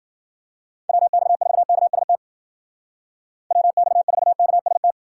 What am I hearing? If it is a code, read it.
Morse code, heard as a keyed tone, about 50 wpm, 700 Hz. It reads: WX4KST WX4KST
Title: Speed Racing - USA Call Signs 40wpm